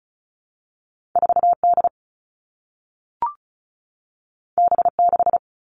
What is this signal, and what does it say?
Morse code, heard as a keyed tone, about 36 wpm, 700 Hz. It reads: 4D B6
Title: All Cut Numbers - 2-Digit 35wpm